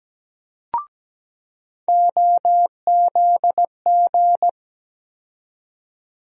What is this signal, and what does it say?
Morse code, heard as a keyed tone, about 17 wpm, 700 Hz. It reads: OZG
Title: USA Call Sign Postfixes - 3 Letters 17wpm